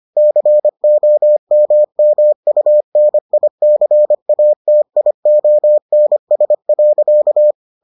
Morse code, heard as a keyed tone, about 25 wpm, 600 Hz. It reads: COMMUNICATIONS.